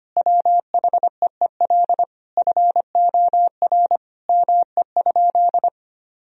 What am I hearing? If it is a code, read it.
Morse code, heard as a keyed tone, about 25 wpm, 700 Hz. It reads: WHEEL FOR ME?
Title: Sentences from Top 1000 Words - Review Part 3 - 25wpm